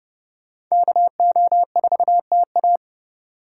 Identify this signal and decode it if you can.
Morse code, heard as a keyed tone, about 38 wpm, 700 Hz. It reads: KO4TA